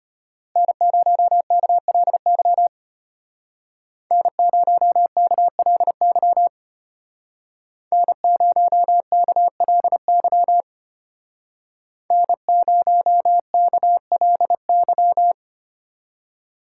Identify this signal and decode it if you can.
Morse code, heard as a keyed tone, about 30 wpm, 700 Hz. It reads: N0KLY N0KLY N0KLY N0KLY
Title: Speed Racing - USA 1x3 Call Signs 25wpm